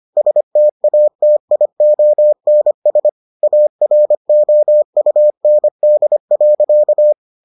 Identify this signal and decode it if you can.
Morse code, heard as a keyed tone, about 25 wpm, 600 Hz. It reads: STATIONS AROUND.